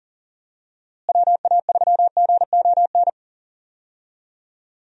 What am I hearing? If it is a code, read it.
Morse code, heard as a keyed tone, about 38 wpm, 700 Hz. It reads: WA3GON
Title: USA Call Signs - 6 Characters - Repeated 2x - 40wpm